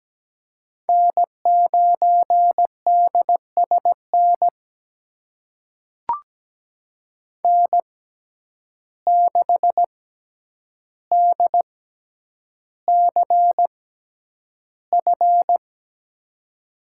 Morse code, heard as a keyed tone, about 17 wpm, spaced as for 9 wpm, 700 Hz. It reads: N9DSN N6DCF
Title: USA Call Signs - 5 Characters - 6x Character Spacing - 17wpm